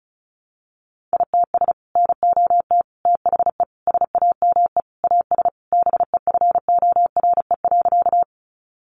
Morse code, heard as a keyed tone, about 34 wpm, 700 Hz. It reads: ITS NOT THE SAME AS BEFORE.
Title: Sentences from Top 600 Words 35wpm